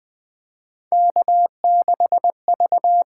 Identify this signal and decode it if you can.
Morse code, heard as a keyed tone, about 20 wpm, 700 Hz. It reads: K6V
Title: USA Call Signs - 8x Character Spacing - 20wpm